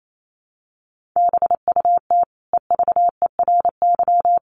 Morse code, heard as a keyed tone, about 28 wpm, 700 Hz. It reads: BUT EVERY
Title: Sentences from Top 1400 Words 28wpm